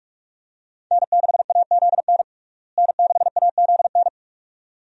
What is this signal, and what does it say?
Morse code, heard as a keyed tone, about 41 wpm, 700 Hz. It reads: N6AZN N6AZN
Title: USA Call Signs - 5 Characters - Repeated 2x - 45wpm